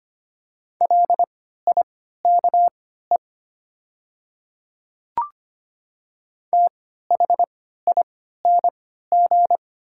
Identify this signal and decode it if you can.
Morse code, heard as a keyed tone, about 25 wpm, spaced as for 14 wpm, 700 Hz. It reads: LIKE THING
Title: 3x Character Spacing - Top 100 Words - 25wpm